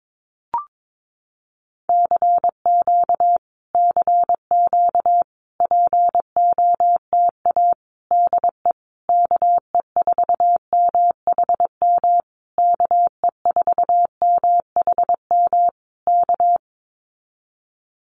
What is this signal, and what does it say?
Morse code, heard as a keyed tone, about 22 wpm, 700 Hz. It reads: CQ CQ POTA DE KE4MHM KE4MHM K
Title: Common QSO Phrases Encore 22wpm